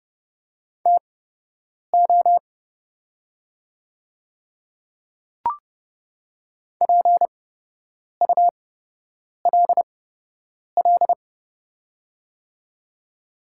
Morse code, heard as a keyed tone, about 30 wpm, spaced as for 8 wpm, 700 Hz. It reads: TO PULL
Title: 8x Character Spacing - Top 500 Words - 30wpm